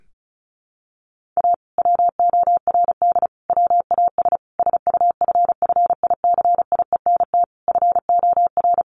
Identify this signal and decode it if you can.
Morse code, heard as a keyed tone, about 34 wpm, 700 Hz. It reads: A WORD WAS SUFFICIENT FOR
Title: Sentences from Top 1400 Words 35wpm